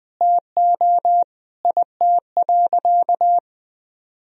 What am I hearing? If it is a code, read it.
Morse code, heard as a keyed tone, about 20 wpm, 700 Hz. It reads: TO IT.
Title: Sentences from Top 1000 Words - Review Part 3 - 20wpm